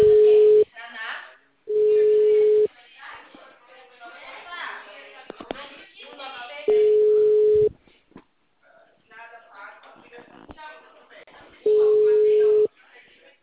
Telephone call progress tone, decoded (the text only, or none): ringback tone